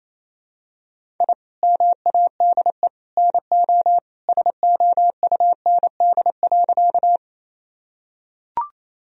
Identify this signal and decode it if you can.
Morse code, heard as a keyed tone, about 27 wpm, 700 Hz. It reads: I MADE NO SOUND.
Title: Sentences from Top 1000 Words - Review Part 1 - 28wpm